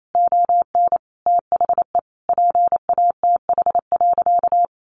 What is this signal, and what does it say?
Morse code, heard as a keyed tone, about 28 wpm, 700 Hz. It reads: ON THE PATH.
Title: A Sound Of Thunder 28wpm